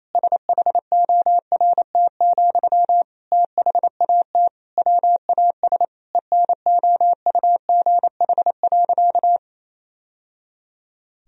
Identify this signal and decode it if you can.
Morse code, heard as a keyed tone, about 28 wpm, 700 Hz. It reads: SHORT, THAT WAS ENOUGH.
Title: Sentences from Top 1000 Words - Review Part 2 - 28wpm